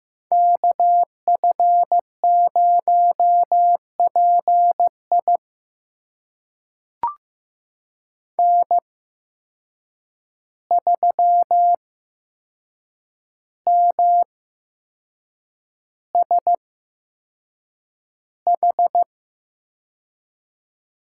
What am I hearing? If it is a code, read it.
Morse code, heard as a keyed tone, about 15 wpm, spaced as for 6 wpm, 700 Hz. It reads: KF0PI N3MSH
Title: USA Call Signs - 5 Characters - 8x Character Spacing - 15wpm